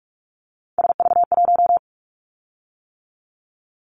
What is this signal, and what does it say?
Morse code, heard as a keyed tone, about 46 wpm, 700 Hz. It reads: SV1
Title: Callsign Prefixes for Top 50 DXCC - 45wpm